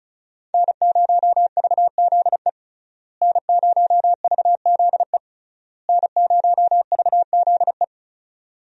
Morse code, heard as a keyed tone, about 34 wpm, 700 Hz. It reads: N0VZE N0VZE N0VZE
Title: USA Call Signs - 5 Characters - Repeated 3x - 35wpm